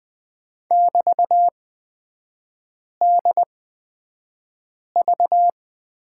Morse code, heard as a keyed tone, about 20 wpm, spaced as for 5 wpm, 700 Hz. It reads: <BT>DV
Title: 224 Lesson 36e - BT - Three Characters 20wpm